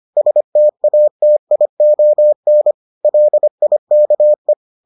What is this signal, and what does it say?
Morse code, heard as a keyed tone, about 25 wpm, 600 Hz. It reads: STATION LIKE